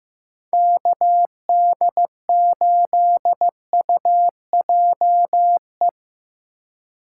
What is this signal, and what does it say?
Morse code, heard as a keyed tone, about 15 wpm, 700 Hz. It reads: KD8UJE
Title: USA Call Signs - 3x Character Spacing - 15wpm